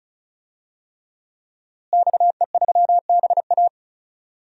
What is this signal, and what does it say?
Morse code, heard as a keyed tone, about 34 wpm, 700 Hz. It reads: XE3BA
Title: DXCC-100 Call Signs - 5 Characters - Repeated 3x - 35wpm